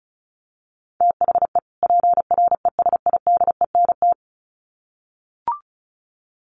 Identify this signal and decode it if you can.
Morse code, heard as a keyed tone, about 34 wpm, 700 Hz. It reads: THE PRESIDENT
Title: Sets of 2 Words 35wpm